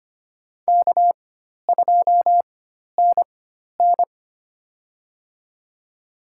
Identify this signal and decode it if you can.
Morse code, heard as a keyed tone, about 25 wpm, spaced as for 12 wpm, 700 Hz. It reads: K2NN
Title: USA Call Signs - 4 Characters - 4x Character Spacing - 25wpm